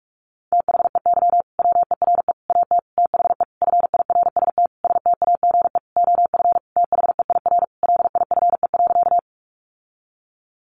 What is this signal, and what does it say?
Morse code, heard as a keyed tone, about 44 wpm, 700 Hz. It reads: THEY WERE AT THE FIRST STAGE OF THEIR LIFE.
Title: Sentences from Top 1200 Words 45wpm